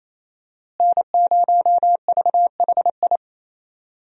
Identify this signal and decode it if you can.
Morse code, heard as a keyed tone, about 28 wpm, 700 Hz. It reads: N0VHI
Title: USA Call Signs - 5 Characters - Repeated 2x - 28wpm